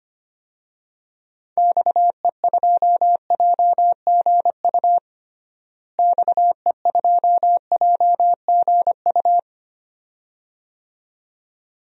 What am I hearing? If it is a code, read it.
Morse code, heard as a keyed tone, about 25 wpm, 700 Hz. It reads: XE2JGU XE2JGU
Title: DXCC-100 Call Signs - 6 Characters - Repeated 2x - 25wpm